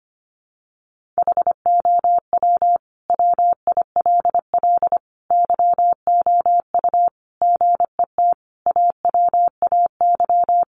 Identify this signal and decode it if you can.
Morse code, heard as a keyed tone, about 25 wpm, 700 Hz. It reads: HOW WILL YOU GET AWAY
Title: Sentences from Top 1500 Words 25wpm